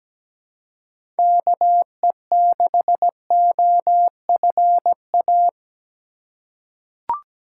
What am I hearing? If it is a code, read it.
Morse code, heard as a keyed tone, about 17 wpm, 700 Hz. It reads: KE6OFA
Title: USA Call Signs - 6 Characters - 3x Character Spacing - 17wpm